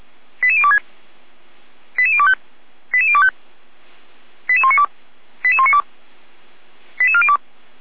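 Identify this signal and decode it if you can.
Five-tone selective call (ZVEI), heard as quick runs of five tones, tones of about 70 ms. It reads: 88026 88025 88025 88192 88192 88402
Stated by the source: ZVEI five-tone selective call received by SDR, from the Feuerwehr Orlamünde fire brigade.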